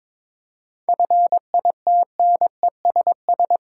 Morse code, heard as a keyed tone, about 22 wpm, 690 Hz. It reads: FITNESS
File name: Difficult Words with Hints - 22wpm